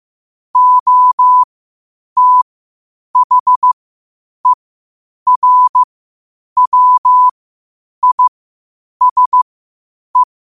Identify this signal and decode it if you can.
Morse code, heard as a keyed tone, about 15 wpm, spaced as for 8 wpm, 1000 Hz. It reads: OTHERWISE
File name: qotdfiles / 8wpm